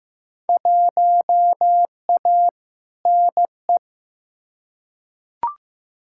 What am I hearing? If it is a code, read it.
Morse code, heard as a keyed tone, about 15 wpm, 700 Hz. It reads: 1A NE
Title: Field Day Exchanges Encore - 15wpm